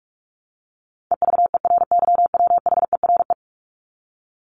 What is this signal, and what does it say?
Morse code, heard as a keyed tone, about 44 wpm, 700 Hz. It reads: EVERYWHERE
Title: Top 1501-1750 Words 45wpm